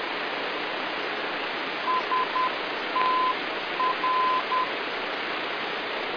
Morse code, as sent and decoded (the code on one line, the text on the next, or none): ... - .-.
STR